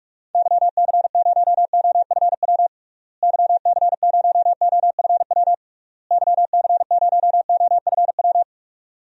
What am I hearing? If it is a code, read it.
Morse code, heard as a keyed tone, about 46 wpm, 700 Hz. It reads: YC0OFW YC0OFW YC0OFW
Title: DXCC-100 Call Signs - 6 Characters - Repeated 3x - 45wpm